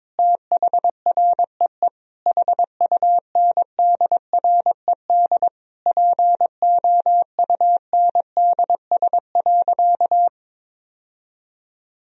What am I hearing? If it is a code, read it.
Morse code, heard as a keyed tone, about 22 wpm, 700 Hz. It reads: THREE HUNDRED POUNDS.
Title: Sentences from Top 1000 Words - Review Part 2 - 22wpm